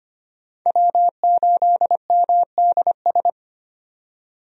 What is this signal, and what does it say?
Morse code, heard as a keyed tone, about 25 wpm, 700 Hz. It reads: W8MDS
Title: USA Call Signs - 5 Characters - 25wpm